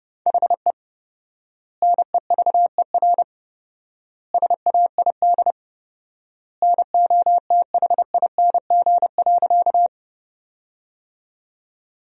Morse code, heard as a keyed tone, about 32 wpm, 700 Hz. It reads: HE NEVER SAID NOTHING.
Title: 4x Word-Spacing - Sentences from Top 500 Words - 30wpm